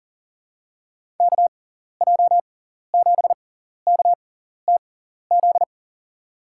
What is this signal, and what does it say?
Morse code, heard as a keyed tone, about 39 wpm, spaced as for 14 wpm, 700 Hz. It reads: KJ7KTZ
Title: USA Call Signs - 6x Character Spacing - 40wpm